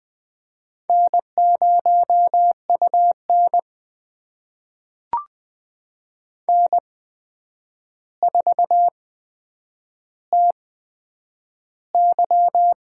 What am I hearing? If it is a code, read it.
Morse code, heard as a keyed tone, about 20 wpm, spaced as for 9 wpm, 700 Hz. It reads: N0UN N4TY
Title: USA Call Signs - 4 Characters - 8x Character Spacing - 20wpm